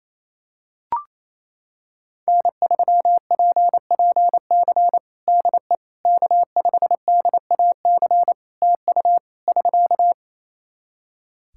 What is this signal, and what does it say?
Morse code, heard as a keyed tone, about 28 wpm, 700 Hz. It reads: N3PPC DE K5DAC TU <SK>
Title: K1USN SST QSO Practice 28wpm